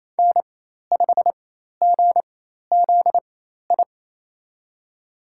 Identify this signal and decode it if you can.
Morse code, heard as a keyed tone, about 28 wpm, spaced as for 13 wpm, 700 Hz. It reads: N5GZI
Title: USA Call Signs - 4x Character Spacing - 28wpm